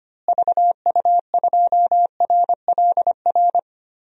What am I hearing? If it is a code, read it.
Morse code, heard as a keyed tone, about 25 wpm, 700 Hz. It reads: VU2RLR
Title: DXCC-100 Call Signs - 6 Characters - Repeated 3x - 25wpm